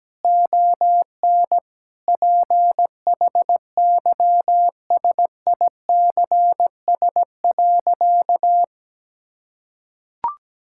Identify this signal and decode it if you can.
Morse code, heard as a keyed tone, about 17 wpm, 700 Hz. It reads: ON PHYSICS.
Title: Sentences from Top 1000 Words - Review Part 5 - 17wpm